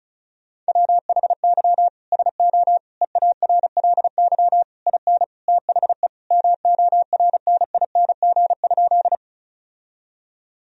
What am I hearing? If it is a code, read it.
Morse code, heard as a keyed tone, about 34 wpm, 700 Hz. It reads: WHY SO EARLY IN THE MORNING?
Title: Sentences from Top 1000 Words - Review Part 2 - 35wpm